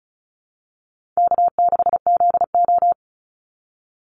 Morse code, heard as a keyed tone, about 34 wpm, 700 Hz. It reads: K6ZO
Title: Easy Call Signs Repeated 3x - 35wpm